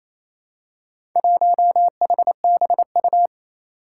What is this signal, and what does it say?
Morse code, heard as a keyed tone, about 27 wpm, 700 Hz. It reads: 1HBU
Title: Four Challenging Letter-Number Combinations 28wpm